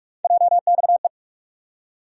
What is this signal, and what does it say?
Morse code, heard as a keyed tone, about 45 wpm, 700 Hz. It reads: JXE
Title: USA Call Sign Postfixes - 3 Letters 45wpm